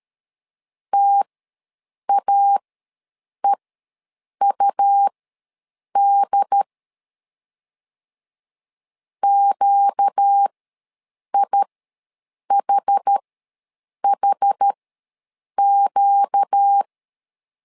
Morse code, heard as a keyed tone, about 13 wpm, spaced as for 7 wpm, 800 Hz. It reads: TAEUD QIHHQ
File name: LEZIONI / 4 Q ws 12wpm 1912